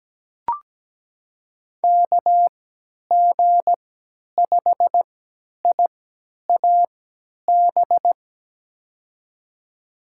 Morse code, heard as a keyed tone, about 17 wpm, spaced as for 10 wpm, 700 Hz. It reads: KG5IAB
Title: USA Call Signs - 6 Characters - 3x Character Spacing - 17wpm